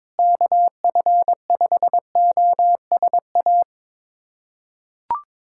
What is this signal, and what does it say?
Morse code, heard as a keyed tone, about 22 wpm, 700 Hz. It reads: KF5OSA
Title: USA Callsign DE Callsign - 6 Characters - 22wpm